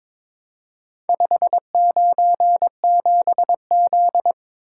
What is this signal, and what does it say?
Morse code, heard as a keyed tone, about 22 wpm, 700 Hz. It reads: 597Z
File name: practice / 22wpm / Four Letter-Number Combinations - 22wpm